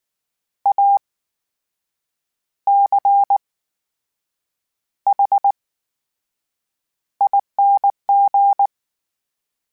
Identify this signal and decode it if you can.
Morse code, heard as a keyed tone, about 19 wpm, spaced as for 7 wpm, 800 Hz. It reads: ACHING
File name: Special INGs / Special INGs-mp3s / VerySlowToFaster Ing @18